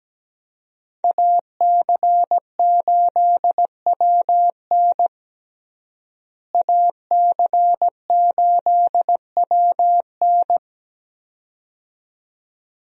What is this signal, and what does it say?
Morse code, heard as a keyed tone, about 17 wpm, 700 Hz. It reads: AC8WN AC8WN
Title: USA Call Signs - 5 Characters - Repeated 2x - 17wpm